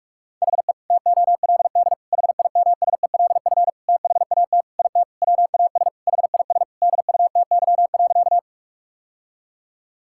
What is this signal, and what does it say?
Morse code, heard as a keyed tone, about 46 wpm, 700 Hz. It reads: HE TOLD HIMSELF THAT IT WAS HIS DUTY.
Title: Sentences from Top 1600 Words 45wpm